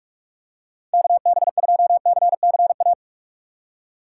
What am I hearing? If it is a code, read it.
Morse code, heard as a keyed tone, about 46 wpm, 700 Hz. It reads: KB2CCA
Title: USA Call Signs - 6 Characters - 6x Character Spacing - 45wpm